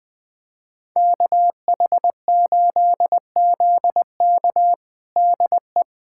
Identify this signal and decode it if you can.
Morse code, heard as a keyed tone, about 20 wpm, 700 Hz. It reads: KH8ZK DE
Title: USA Callsign DE Callsign - 5 Characters - 20wpm